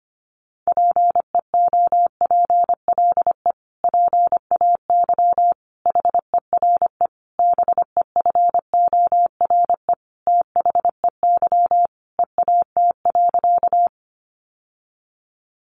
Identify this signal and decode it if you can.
Morse code, heard as a keyed tone, about 25 wpm, 700 Hz. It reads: PEOPLE PAY HERE BEFORE THEY EAT.
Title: Sentences from Top 700 Words 25wpm